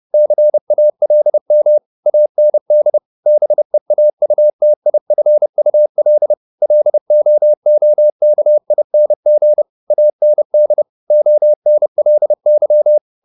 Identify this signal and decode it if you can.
Morse code, heard as a keyed tone, about 30 wpm, 600 Hz. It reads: CALM AND BEAUTIFUL LOOKING AND ONLY